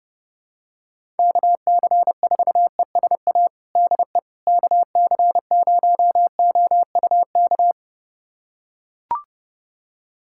Morse code, heard as a keyed tone, about 32 wpm, 700 Hz. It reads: KC4ESA DE KC0OUK
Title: USA Callsign DE Callsign - 6 Characters - 30wpm